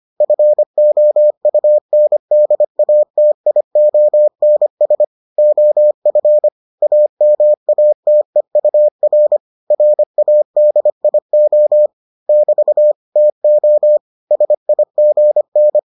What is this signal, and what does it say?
Morse code, heard as a keyed tone, about 25 wpm, 600 Hz. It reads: FOUNDATIONS OF AMATEUR RADIO <BT> TO SIGN